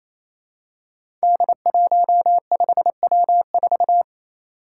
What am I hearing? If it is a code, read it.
Morse code, heard as a keyed tone, about 29 wpm, 700 Hz. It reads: D15W4
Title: Five Challenging Letter-Number Combinations 28wpm